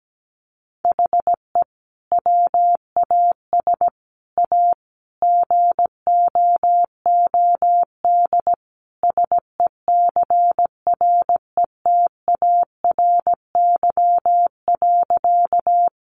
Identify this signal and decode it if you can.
Morse code, heard as a keyed tone, about 17 wpm, 700 Hz. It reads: HE WAS A GOOD SECRETARY.
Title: Sentences from Top 1300 Words 17wpm